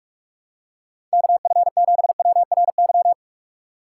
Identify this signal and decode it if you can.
Morse code, heard as a keyed tone, about 42 wpm, 700 Hz. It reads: KF7WRY
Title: USA Call Signs - 6 Characters - 4x Character Spacing - 45wpm